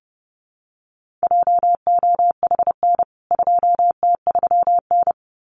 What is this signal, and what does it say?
Morse code, heard as a keyed tone, about 30 wpm, 700 Hz. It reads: JOHN 2T3N
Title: CWT Exchange 30wpm